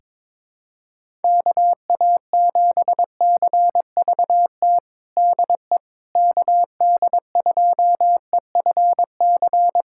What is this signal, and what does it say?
Morse code, heard as a keyed tone, about 22 wpm, 700 Hz. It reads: KA7CVT DE KD2EFC
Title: USA Callsign DE Callsign 22wpm